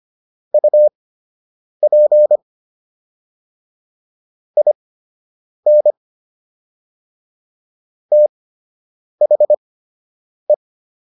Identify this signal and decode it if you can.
Morse code, heard as a keyed tone, about 25 wpm, spaced as for 8 wpm, 600 Hz. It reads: UP IN THE